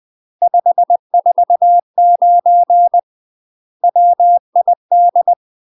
Morse code, heard as a keyed tone, about 20 wpm, 700 Hz. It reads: 549 WID
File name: UK MASTER QSOs TRAINING / 20WPM / UK MASTER QSOs 0044